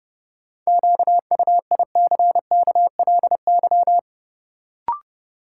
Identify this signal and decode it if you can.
Morse code, heard as a keyed tone, about 30 wpm, 700 Hz. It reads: QUICKLY
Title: Sentences from Top 700 Words - New Word Drill - 30wpm